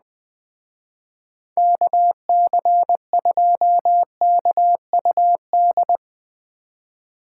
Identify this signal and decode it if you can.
Morse code, heard as a keyed tone, about 20 wpm, 700 Hz. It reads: KC2KUD
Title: Speed Racing - USA 2x3 Call Signs 17wpm